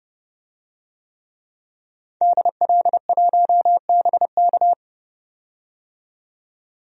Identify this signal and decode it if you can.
Morse code, heard as a keyed tone, about 30 wpm, 700 Hz. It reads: DL1BK
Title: DXCC-100 Call Signs - 5 Characters - 30wpm